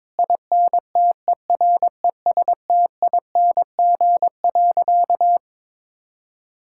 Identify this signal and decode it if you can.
Morse code, heard as a keyed tone, about 22 wpm, 700 Hz. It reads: INTERESTING.
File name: practice / 22wpm / Sentences with 3-Word Phrases - 22wpm